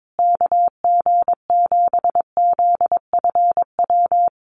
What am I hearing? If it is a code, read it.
Morse code, heard as a keyed tone, about 22 wpm, 700 Hz. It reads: KG7ZFW
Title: USA Call Signs Repeated 2x 22wpm